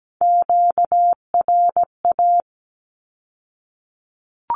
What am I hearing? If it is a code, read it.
Morse code, heard as a keyed tone, about 17 wpm, 700 Hz. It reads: QRA
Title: Q-Codes with Spoken Meaning - 17wpm